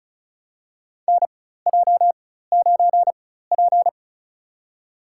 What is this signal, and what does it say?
Morse code, heard as a keyed tone, about 36 wpm, spaced as for 16 wpm, 700 Hz. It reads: NJ9P
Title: USA Call Signs - 4x Character Spacing - 35wpm